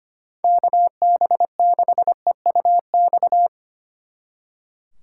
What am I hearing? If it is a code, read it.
Morse code, heard as a keyed tone, about 28 wpm, 700 Hz. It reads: KB6EUX
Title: Speed Racing - USA 2x3 Call Signs 25wpm